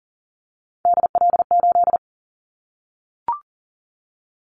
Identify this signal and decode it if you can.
Morse code, heard as a keyed tone, about 40 wpm, 700 Hz. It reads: DL8
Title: Callsign Prefixes for Top 25 DXCC - Repeated 3x - 40wpm